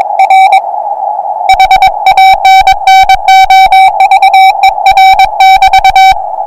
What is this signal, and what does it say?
Morse code, heard as a keyed tone, about 21 wpm, 770 Hz. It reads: R HANNOVER <BT>
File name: attachments / 40m up dn